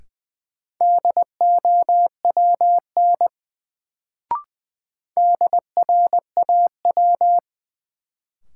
Morse code, heard as a keyed tone, about 20 wpm, 700 Hz. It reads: DOWN DRAW
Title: Sentences from Top 300 Words - New Word Drill - 20wpm